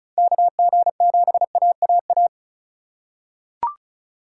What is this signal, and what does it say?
Morse code, heard as a keyed tone, about 36 wpm, 700 Hz. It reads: KG7AAA
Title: USA Call Signs - 6 Characters - 3x Character Spacing - 35wpm